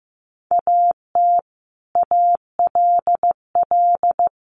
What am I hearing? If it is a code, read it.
Morse code, heard as a keyed tone, about 15 wpm, 700 Hz. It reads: AT ALL
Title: Sets of 2 Words 15wpm